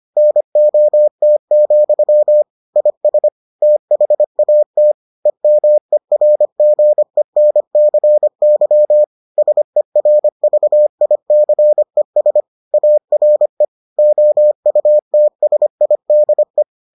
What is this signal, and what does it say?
Morse code, heard as a keyed tone, about 25 wpm, 600 Hz. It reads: NOT, IS THAT EMERGENCY SERVICES ARE OUTSIDE